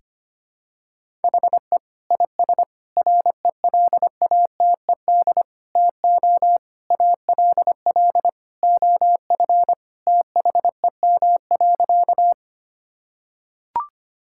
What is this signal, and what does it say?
Morse code, heard as a keyed tone, about 25 wpm, 700 Hz. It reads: HE IS RELATED TO ALL OF THEM.
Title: Sentences from Top 1000 Words - Review Part 5 - 25wpm